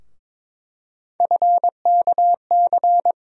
Morse code, heard as a keyed tone, about 22 wpm, 700 Hz. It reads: FKC
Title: USA Call Sign Postfixes - 3 Letters 22wpm